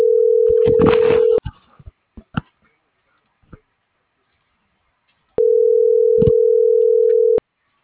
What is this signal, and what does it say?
Ringback tone (North America), 440 + 480 Hz.